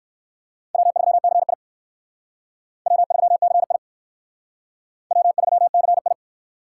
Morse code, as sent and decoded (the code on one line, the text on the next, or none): .-- ...-- -.-. .. / .-- ...-- -.-. .. / .-- ...-- -.-. ..
W3CI W3CI W3CI